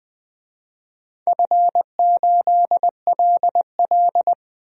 Morse code, heard as a keyed tone, about 20 wpm, 700 Hz. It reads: F8LL
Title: DXCC-100 Call Signs - 4 Characters - 20wpm